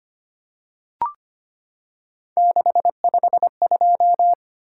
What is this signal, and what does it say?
Morse code, heard as a keyed tone, about 25 wpm, 700 Hz. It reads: 652
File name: Three Letter-Number Combinations - 25wpm